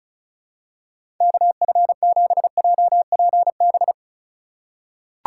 Morse code, heard as a keyed tone, about 34 wpm, 700 Hz. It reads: KF7JPB
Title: USA Call Signs - 6 Characters - 2x Character Spacing - 35wpm